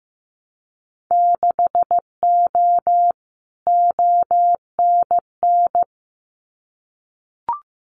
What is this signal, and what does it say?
Morse code, heard as a keyed tone, about 15 wpm, 700 Hz. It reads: 6O ONN
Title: Winter Field Day Exchanges Encore 15wpm